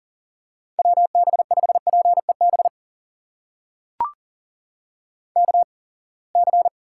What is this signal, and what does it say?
Morse code, heard as a keyed tone, about 40 wpm, spaced as for 28 wpm, 700 Hz. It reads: WB5PEB KC
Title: USA Call Signs - 6 Characters - 8x Character Spacing - 40wpm